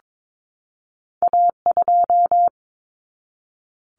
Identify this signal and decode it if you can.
Morse code, heard as a keyed tone, about 22 wpm, 700 Hz. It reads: A2
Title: Common Cut Numbers - 2-Digit 22wpm